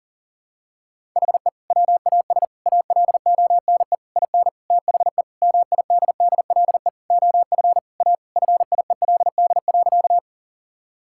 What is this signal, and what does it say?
Morse code, heard as a keyed tone, about 40 wpm, 700 Hz. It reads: HE WAS ALONE IN THE MIDDLE OF A FIELD.
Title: Sentences from Top 1000 Words - Review Part 6 - 40wpm